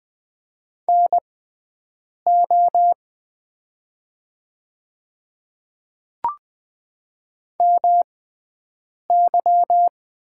Morse code, heard as a keyed tone, about 20 wpm, spaced as for 7 wpm, 700 Hz. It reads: NO MY
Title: 6x Character Spacing - Top 200 Words - 20wpm